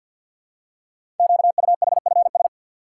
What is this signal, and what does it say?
Morse code, heard as a keyed tone, about 50 wpm, 700 Hz. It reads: QUHFS
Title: Code Groups - 5 Letters - 50wpm